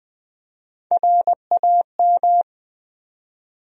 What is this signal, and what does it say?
Morse code, heard as a keyed tone, about 20 wpm, 700 Hz. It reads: RAM
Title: USA Call Sign Postfixes - 3 Letters 20wpm